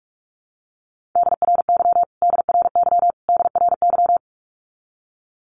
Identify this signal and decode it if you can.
Morse code, heard as a keyed tone, about 44 wpm, 700 Hz. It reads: DRY DRY DRY
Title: Top 500 Words Repeated 45wpm